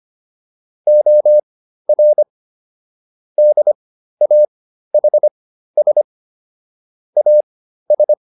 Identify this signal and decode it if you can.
Morse code, heard as a keyed tone, about 25 wpm, spaced as for 13 wpm, 600 Hz. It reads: OR DAHS AS